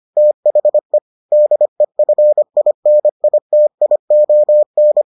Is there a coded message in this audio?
Yes: Morse code from 0.2 to 5.0 s, heard as a keyed tone, about 25 wpm, 600 Hz. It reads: THE DEFINITION